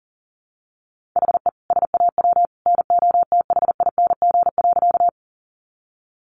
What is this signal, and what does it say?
Morse code, heard as a keyed tone, about 40 wpm, 700 Hz. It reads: HE SAW NOTHING.
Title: Sentences from Top 500 Words 40wpm